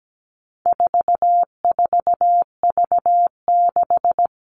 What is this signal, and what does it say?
Morse code, heard as a keyed tone, about 17 wpm, 700 Hz. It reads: 44V6
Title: All Cut Numbers - 4-Digit 17wpm